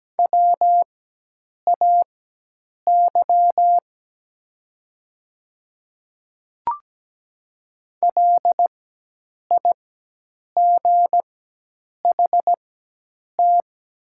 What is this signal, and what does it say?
Morse code, heard as a keyed tone, about 17 wpm, spaced as for 8 wpm, 700 Hz. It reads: WAY LIGHT
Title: 4x Character Spacing - Top 200 Words - 17wpm